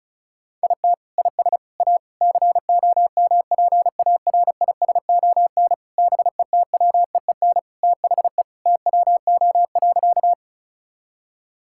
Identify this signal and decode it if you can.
Morse code, heard as a keyed tone, about 34 wpm, 700 Hz. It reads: IT IS A COMPARISON BETWEEN THE TWO.
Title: Sentences from Top 1600 Words 35wpm